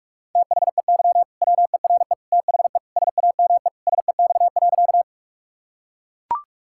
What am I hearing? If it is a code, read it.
Morse code, heard as a keyed tone, about 42 wpm, 700 Hz. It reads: THEY WERE THE SAME SEX.
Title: Sentences from Top 1600 Words 45wpm